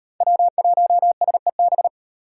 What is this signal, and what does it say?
Morse code, heard as a keyed tone, about 37 wpm, 700 Hz. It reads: W1SEB